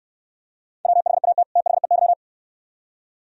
Morse code, heard as a keyed tone, about 69 wpm, 690 Hz. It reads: WHAT THEY